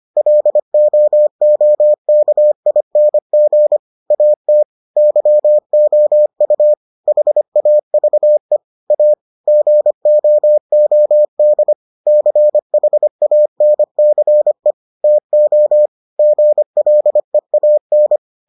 Morse code, heard as a keyed tone, about 25 wpm, 600 Hz. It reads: LOOKING AT YOU HAVE A GOOD CHANCE TO GLEAN